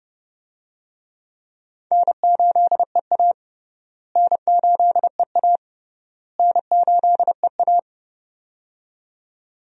Morse code, heard as a keyed tone, about 30 wpm, 700 Hz. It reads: N8EA N8EA N8EA
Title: USA Call Signs - 4 Characters - Repeated 3x - 30wpm